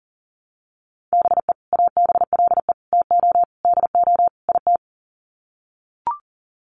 Morse code, heard as a keyed tone, about 40 wpm, 700 Hz. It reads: BE ABLE TO DO IT
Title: Sets of 5 Words Encore2 40wpm